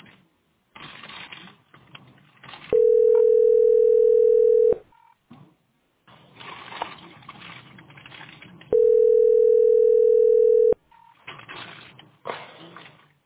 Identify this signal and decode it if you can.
Ringback tone (North America), 440 + 480 Hz.